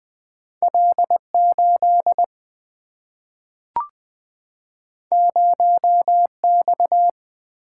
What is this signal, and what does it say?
Morse code, heard as a keyed tone, about 20 wpm, 700 Hz. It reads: L8 0X
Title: Single-Letter Single-Number 20wpm